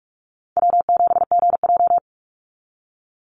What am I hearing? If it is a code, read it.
Morse code, heard as a keyed tone, about 46 wpm, 690 Hz. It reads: W7GJ
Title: NA SOTA Chaser Callsigns 45wpm